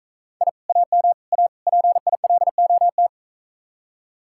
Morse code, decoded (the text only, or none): I AM A PILOT